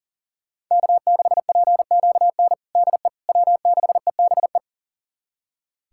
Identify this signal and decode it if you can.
Morse code, heard as a keyed tone, about 37 wpm, 690 Hz. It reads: K6PQN DE W6EBE
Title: USA Callsign DE Callsign - 5 Characters - 40wpm